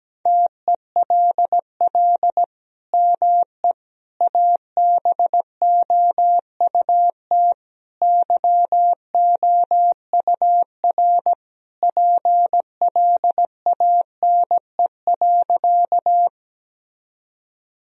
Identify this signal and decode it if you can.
Morse code, heard as a keyed tone, about 17 wpm, 700 Hz. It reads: TELL ME ABOUT YOUR PLANE.
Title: Sentences from Top 1000 Words - Review Part 3 - 17wpm